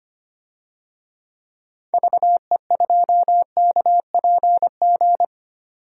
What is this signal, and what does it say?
Morse code, heard as a keyed tone, about 25 wpm, 700 Hz. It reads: VE2KPG
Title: DXCC-100 Call Signs - 6 Characters - 25wpm